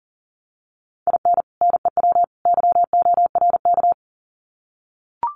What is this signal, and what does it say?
Morse code, heard as a keyed tone, about 40 wpm, 700 Hz. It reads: IN NEW YORK
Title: Sets of 3 Words 40wpm